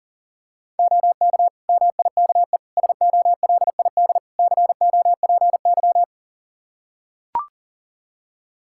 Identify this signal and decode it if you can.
Morse code, heard as a keyed tone, about 37 wpm, 700 Hz. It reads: OK MIKE SOLID COPY
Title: Practice QSO - Exchange 3 - More Info - 40wpm